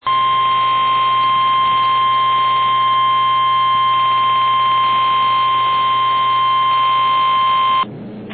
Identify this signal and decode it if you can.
Emergency Alert System (SAME) tones (the 1050 Hz attention tone), with noise. It is heard from 0.0 to 7.9 s.
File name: eas / KNCYKBIE / KNCYKBIE